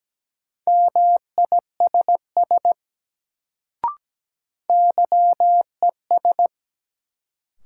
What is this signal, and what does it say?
Morse code, heard as a keyed tone, about 17 wpm, 700 Hz. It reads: MISS YES
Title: Sentences from Top 500 Words - New Word Drill - 17wpm